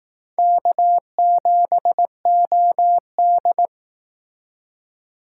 Morse code, heard as a keyed tone, about 18 wpm, 700 Hz. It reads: K7OD
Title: Speed Racing - USA 1x2 Call Signs 15wpm